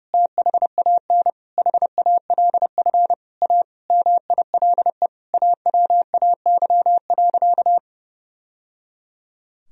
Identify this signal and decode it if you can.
Morse code, heard as a keyed tone, about 30 wpm, 700 Hz. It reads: THAN HALF A MILE AWAY.
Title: Sentences from Top 1000 Words - Review Part 2 - 30wpm